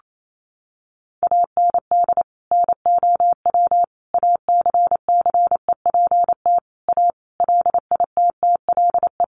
Morse code, heard as a keyed tone, about 28 wpm, 700 Hz. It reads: AND NOW ACCEPT A LITTLE
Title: Sentences from Top 1300 Words 28wpm